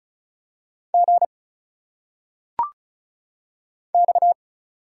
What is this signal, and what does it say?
Morse code, heard as a keyed tone, about 34 wpm, 700 Hz. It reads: G X
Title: Single Letters - Phonetic Pronunciation 35wpm